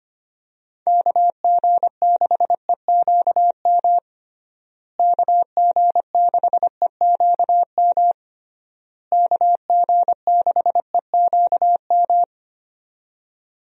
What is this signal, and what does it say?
Morse code, heard as a keyed tone, about 25 wpm, 700 Hz. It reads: KG6EQM KG6EQM KG6EQM
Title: USA Call Signs - 6 Characters - Repeated 3x - 25wpm